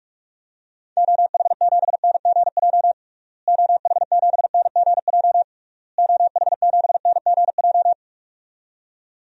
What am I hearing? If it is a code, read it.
Morse code, heard as a keyed tone, about 42 wpm, 700 Hz. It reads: OH7NGJ OH7NGJ OH7NGJ
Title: DXCC-100 Call Signs - 6 Characters - Repeated 3x - 45wpm